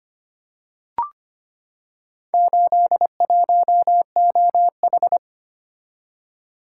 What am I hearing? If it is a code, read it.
Morse code, heard as a keyed tone, about 25 wpm, 700 Hz. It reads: 81OH
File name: practice / 25wpm / Four Letter-Number Combinations - 25wpm